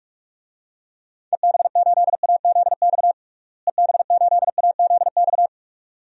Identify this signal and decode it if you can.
Morse code, heard as a keyed tone, about 44 wpm, 700 Hz. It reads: EB8AZX EB8AZX
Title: DXCC-100 Call Signs - 6 Characters - Repeated 2x - 45wpm